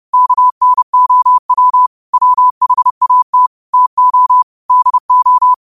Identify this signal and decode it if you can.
Morse code, heard as a keyed tone, about 30 wpm, 1000 Hz. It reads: KNOW WHAT TO DO
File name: qotdfiles / 30wpm